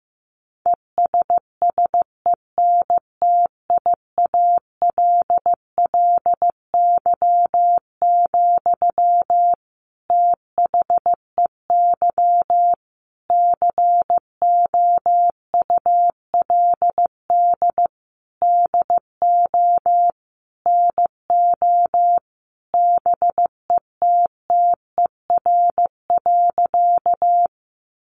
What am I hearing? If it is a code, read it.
Morse code, heard as a keyed tone, about 15 wpm, 700 Hz. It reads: ESSENTIALLY, THEY COULD DO NO BETTER.
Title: Sentences from Top 1300 Words 15wpm